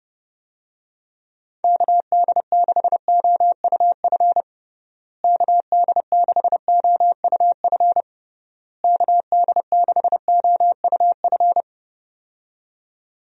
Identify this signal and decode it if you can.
Morse code, heard as a keyed tone, about 32 wpm, 700 Hz. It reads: KD6OUF KD6OUF KD6OUF
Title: USA Call Signs - 6 Characters - Repeated 3x - 30wpm